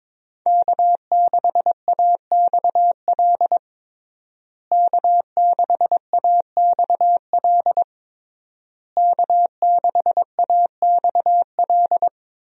Morse code, heard as a keyed tone, about 22 wpm, 700 Hz. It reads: K6AXL K6AXL K6AXL
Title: USA Call Signs - 5 Characters - Repeated 3x - 22wpm